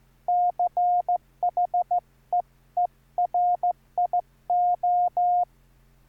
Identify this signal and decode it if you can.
Morse code, heard as a keyed tone, about 14 wpm, 710 Hz. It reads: CHEERIO